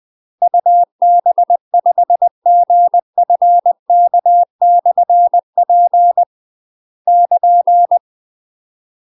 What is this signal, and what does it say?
Morse code, heard as a keyed tone, about 20 wpm, 700 Hz. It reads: UB5GFK/P <KN>
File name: UK MASTER QSOs TRAINING / 20WPM / UK MASTER QSOs 0007